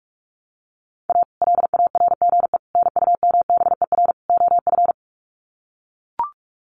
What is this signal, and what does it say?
Morse code, heard as a keyed tone, about 44 wpm, 700 Hz. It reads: A LARGE NUMBER OF
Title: Sets of 4 Words 45wpm